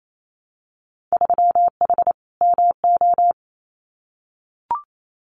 Morse code, heard as a keyed tone, about 28 wpm, 700 Hz. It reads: 3H MO
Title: Winter Field Day Exchanges Encore 28wpm